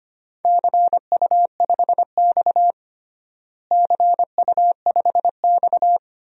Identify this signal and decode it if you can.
Morse code, heard as a keyed tone, about 25 wpm, 700 Hz. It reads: CU5X CU5X
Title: DXCC-100 Call Signs - 4 Characters - Repeated 2x - 25wpm